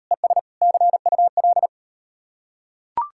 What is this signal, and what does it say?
Morse code, heard as a keyed tone, about 38 wpm, 700 Hz. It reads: ES CUL